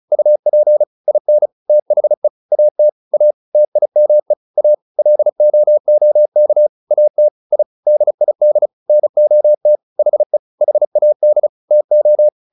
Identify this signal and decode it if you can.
Morse code, heard as a keyed tone, about 34 wpm, 600 Hz. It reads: UP IN THE AT A TIME A LOOK AT I DID NOT HE HAD TO